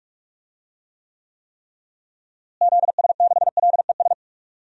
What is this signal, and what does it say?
Morse code, heard as a keyed tone, about 41 wpm, 700 Hz. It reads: ZS6LES
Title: DXCC-100 Call Signs - 6 Characters - Repeated 3x - 45wpm